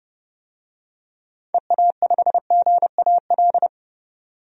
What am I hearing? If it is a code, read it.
Morse code, heard as a keyed tone, about 30 wpm, 700 Hz. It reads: EA5GAL
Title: DXCC-100 Call Signs - 6 Characters - 30wpm